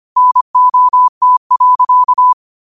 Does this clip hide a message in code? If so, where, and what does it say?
Yes: Morse code from 0.2 to 2.3 s, heard as a keyed tone, about 25 wpm, 1000 Hz. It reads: NOT.